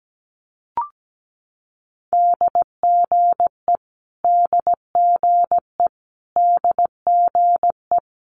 Morse code, heard as a keyed tone, about 17 wpm, 700 Hz. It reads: DGE DGE DGE
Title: Multi-letter Phonemes Repeated 3x 17wpm